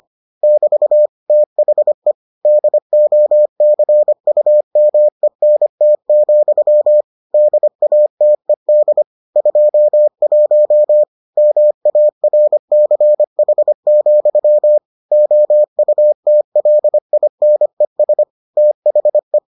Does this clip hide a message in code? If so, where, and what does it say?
Yes: Morse code from 0.4 to 19.4 s, heard as a keyed tone, about 25 wpm, 600 Hz. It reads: <BT> THE DOCUMENT, DATED 21 MARCH, OUTLINES THE